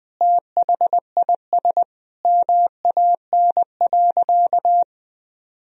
Morse code, heard as a keyed tone, about 20 wpm, 700 Hz. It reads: THIS MAN.
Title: Sentences from Top 1000 Words - Review Part 1 - 20wpm